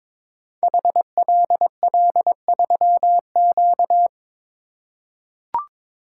Morse code, heard as a keyed tone, about 23 wpm, 700 Hz. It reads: HLL3Q
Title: Code Groups - 5 Letter:Numbers - 22wpm